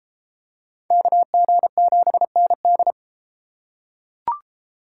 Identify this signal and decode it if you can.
Morse code, heard as a keyed tone, about 30 wpm, 700 Hz. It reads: KG7ND